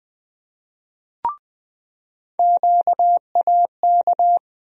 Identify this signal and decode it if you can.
Morse code, heard as a keyed tone, about 20 wpm, 700 Hz. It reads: QAK
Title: USA Call Sign Postfixes - 3 Letters 20wpm